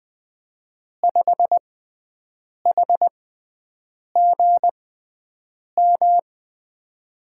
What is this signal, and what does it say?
Morse code, heard as a keyed tone, about 20 wpm, spaced as for 7 wpm, 700 Hz. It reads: 5HGM
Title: USA Call Signs - 5 Characters - 6x Character Spacing - 20wpm